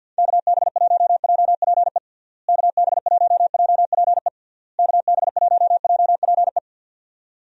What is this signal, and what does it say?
Morse code, heard as a keyed tone, about 51 wpm, 700 Hz. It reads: KB1JPE KB1JPE KB1JPE
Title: USA Call Signs - 6 Characters - Repeated 3x - 50wpm